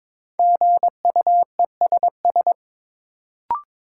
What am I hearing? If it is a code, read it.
Morse code, heard as a keyed tone, about 22 wpm, 700 Hz. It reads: GUESS
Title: Sentences from Top 1500 Words - New Word Drill - 22wpm